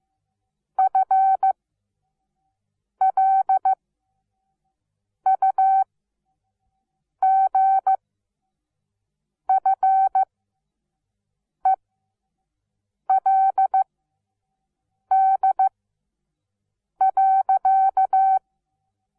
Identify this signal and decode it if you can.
Morse code, heard as a keyed tone, about 15 wpm, spaced as for 5 wpm, 770 Hz. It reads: FLUGFELD.
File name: Onlinelehrgang / morsen / DJ4UF Morsen17